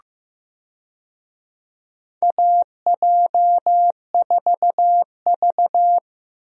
Morse code, heard as a keyed tone, about 15 wpm, 700 Hz. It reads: AJ4V
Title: USA Call Signs - 4 Characters - Repeated 3x - 15wpm